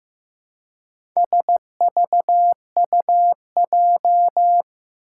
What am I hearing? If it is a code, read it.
Morse code, heard as a keyed tone, about 15 wpm, 700 Hz. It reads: SVUJ
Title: Four Challenging Letter-Number Combinations 15wpm